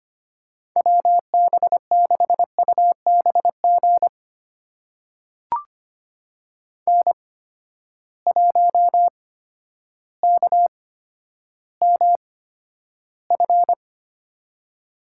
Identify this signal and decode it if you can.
Morse code, heard as a keyed tone, about 25 wpm, spaced as for 11 wpm, 700 Hz. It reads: WB6UBG N1KMF
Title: USA Call Signs - 8x Character Spacing - 25wpm